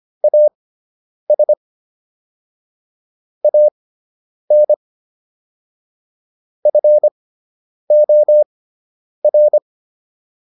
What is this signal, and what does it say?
Morse code, heard as a keyed tone, about 25 wpm, spaced as for 9 wpm, 600 Hz. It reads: AS AN FOR